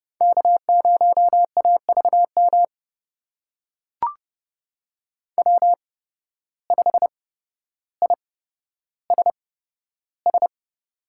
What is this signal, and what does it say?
Morse code, heard as a keyed tone, about 30 wpm, spaced as for 13 wpm, 700 Hz. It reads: K0AVM W5ISS
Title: USA Call Signs - 5 Characters - 8x Character Spacing - 30wpm